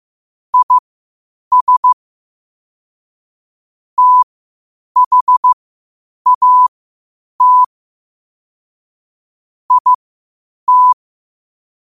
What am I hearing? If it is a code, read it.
Morse code, heard as a keyed tone, about 15 wpm, spaced as for 8 wpm, 1000 Hz. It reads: IS THAT IT
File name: qotdfiles / 8wpm